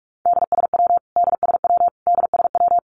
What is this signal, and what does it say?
Morse code, heard as a keyed tone, about 44 wpm, 700 Hz. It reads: DSW DSW DSW
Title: Abbreviations Repeated 3x - 45wpm